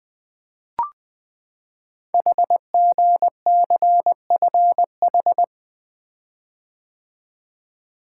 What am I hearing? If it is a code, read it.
Morse code, heard as a keyed tone, about 20 wpm, 700 Hz. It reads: HGCFH
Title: Code Groups - 5 Letters - 20wpm